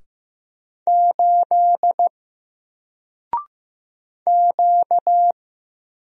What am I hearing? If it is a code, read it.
Morse code, heard as a keyed tone, about 15 wpm, 700 Hz. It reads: 8 Q
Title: Single Letter-Number Alt 15wpm